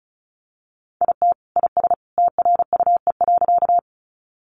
Morse code, heard as a keyed tone, about 34 wpm, 700 Hz. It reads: IT IS TRUE.
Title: Sentences from Top 400 Words 35wpm